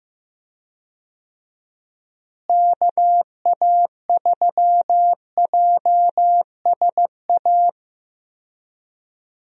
Morse code, heard as a keyed tone, about 15 wpm, 700 Hz. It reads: KA3JSA
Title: USA Call Signs - 6 Characters - Repeated 3x - 15wpm